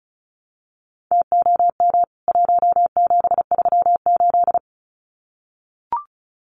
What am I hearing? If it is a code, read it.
Morse code, heard as a keyed tone, about 34 wpm, 700 Hz. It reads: TOM 1738
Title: CWT Exchange 35wpm